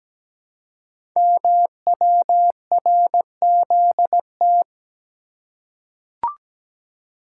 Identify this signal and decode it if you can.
Morse code, heard as a keyed tone, about 17 wpm, 700 Hz. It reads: MWRZT
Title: Code Groups - 5 Letters - 17wpm